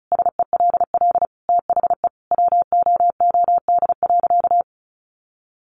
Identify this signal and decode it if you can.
Morse code, heard as a keyed tone, about 34 wpm, 700 Hz. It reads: SELL THE WOOD.
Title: Sentences from Top 1100 Words 35wpm